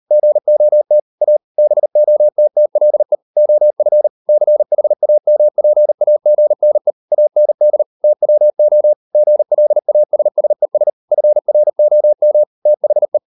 Morse code, heard as a keyed tone, about 38 wpm, 600 Hz. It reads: GOT A BOTTLE OF CHAMPAGNE AND TWO GLASSES FROM THE